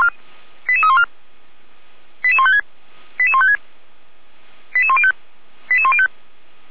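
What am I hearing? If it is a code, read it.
Five-tone selective call (ZVEI), heard as quick runs of five tones, tones of about 70 ms. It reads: 88314 88156 88156 88105 88105
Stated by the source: ZVEI five-tone selective call 88119, received by SDR.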